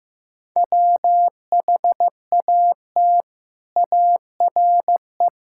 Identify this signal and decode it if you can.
Morse code, heard as a keyed tone, about 15 wpm, 700 Hz. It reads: WHAT ARE
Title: Sentences from Top 1000 Words - Review Part 5 - 15wpm